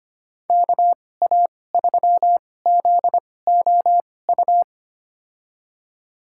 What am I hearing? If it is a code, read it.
Morse code, heard as a keyed tone, about 25 wpm, spaced as for 18 wpm, 700 Hz. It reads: KA3ZOU
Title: USA Call Signs - 6 Characters - 2x Character Spacing - 25wpm